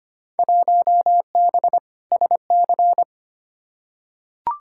Morse code, heard as a keyed tone, about 25 wpm, 700 Hz. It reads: 1B SC
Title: Field Day Exchanges - 25wpm